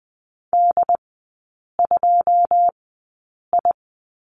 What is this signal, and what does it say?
Morse code, heard as a keyed tone, about 19 wpm, spaced as for 8 wpm, 700 Hz. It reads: D2I
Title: 081 Lesson 14e - Number 2 - Three Characters 20wpm